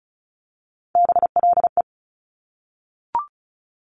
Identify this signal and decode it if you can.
Morse code, heard as a keyed tone, about 36 wpm, 700 Hz. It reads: BLE
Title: Top Three Letter Combinations in Words 35wpm